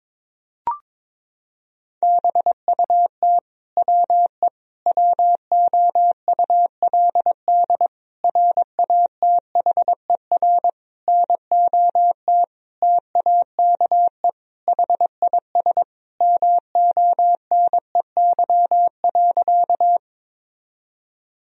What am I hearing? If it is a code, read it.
Morse code, heard as a keyed tone, about 22 wpm, 700 Hz. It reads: BUT WE WOULD RATHER NOT TAKE HIS MONEY.
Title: Sentences from Top 1000 Words - Review Part 4 - 22wpm